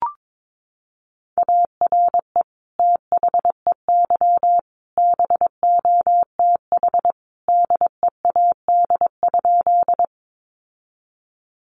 Morse code, heard as a keyed tone, about 22 wpm, 700 Hz. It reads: ARE THEY BOTH DEAD?
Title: Sentences from Top 1500 Words 22wpm